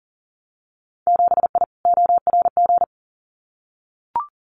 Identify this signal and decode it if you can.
Morse code, heard as a keyed tone, about 40 wpm, 700 Hz. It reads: 7I ORG
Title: Winter Field Day Exchanges Encore 40wpm